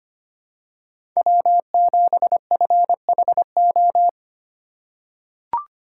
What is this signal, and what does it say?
Morse code, heard as a keyed tone, about 25 wpm, 700 Hz. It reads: W7FHO
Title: USA Call Signs - 5 Characters - 3x Character Spacing - 25wpm